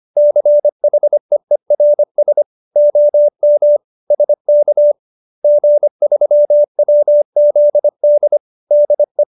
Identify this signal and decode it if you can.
Morse code, heard as a keyed tone, about 25 wpm, 590 Hz. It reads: CHEERS OM SK G3WZD DE